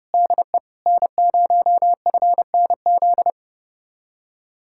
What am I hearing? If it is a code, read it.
Morse code, heard as a keyed tone, about 33 wpm, 700 Hz. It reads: DE N0FNZ